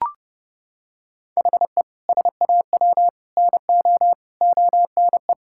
Morse code, heard as a keyed tone, about 30 wpm, 700 Hz. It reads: HE SAW NO ONE
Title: Sentences from Top 1000 Words - Review Part 6 - 30wpm